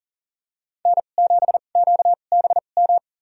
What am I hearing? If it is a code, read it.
Morse code, heard as a keyed tone, about 40 wpm, spaced as for 29 wpm, 700 Hz. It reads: N7QBM
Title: USA Call Signs - 5 Characters - 2x Character Spacing - 40wpm